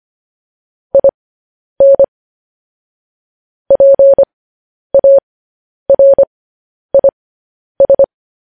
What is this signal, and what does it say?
Morse code, heard as a keyed tone, about 25 wpm, spaced as for 10 wpm, 550 Hz. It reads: IN PARIS
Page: news headlines in morse code at 10 wpm